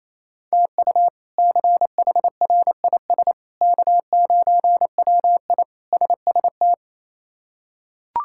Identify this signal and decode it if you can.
Morse code, heard as a keyed tone, about 28 wpm, 700 Hz. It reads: TU CHRIS K9WI SST
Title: K1USN SST QSO Practice - 28wpm